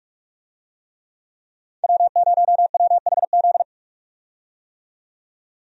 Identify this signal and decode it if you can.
Morse code, heard as a keyed tone, about 43 wpm, 700 Hz. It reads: W0WHZ
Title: USA Call Signs - 5 Characters - 45wpm